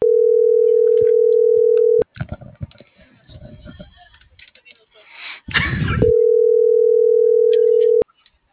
Ringback tone (North America), 440 + 480 Hz.